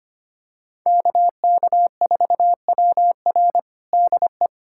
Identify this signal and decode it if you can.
Morse code, heard as a keyed tone, about 25 wpm, 700 Hz. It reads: KK4WR DE
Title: USA Callsign DE Callsign - 5 Characters - 25wpm